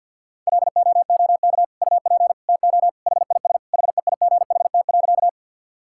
Morse code, heard as a keyed tone, about 50 wpm, 700 Hz. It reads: LOOK UP TO HIS HEIGHT.
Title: Sentences from Top 1600 Words 50wpm